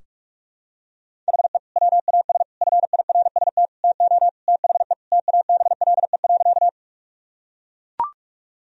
Morse code, heard as a keyed tone, about 41 wpm, 700 Hz. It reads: HE WAS FIRST TO THE TABLE.
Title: Sentences from Top 1000 Words - Review Part 2 - 45wpm